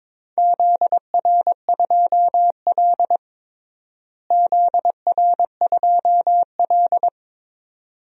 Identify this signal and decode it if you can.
Morse code, heard as a keyed tone, about 22 wpm, 700 Hz. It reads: ZR2L ZR2L
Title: DXCC-100 Call Signs - 4 Characters - Repeated 2x - 22wpm